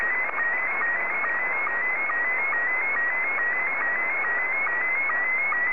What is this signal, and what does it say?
SSTV (slow-scan television), Scottie 1.
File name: sstv / S1-ZL2TJM